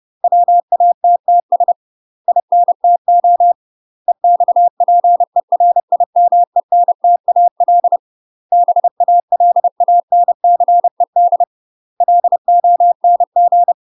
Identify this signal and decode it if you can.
Morse code, heard as a keyed tone, about 30 wpm, 700 Hz. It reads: WATTS INTO EXPERIMENTAL BALANCED LONG
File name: UK MASTER QSOs TRAINING / 30WPM / UK MASTER QSOs 0011